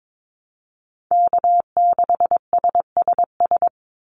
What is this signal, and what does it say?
Morse code, heard as a keyed tone, about 22 wpm, 700 Hz. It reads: K6SSS
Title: Call Signs - 22wpm